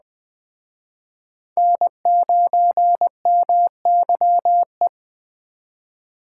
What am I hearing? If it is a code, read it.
Morse code, heard as a keyed tone, about 20 wpm, 700 Hz. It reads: N9MYE
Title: Speed Racing - USA 1x3 Call Signs 17wpm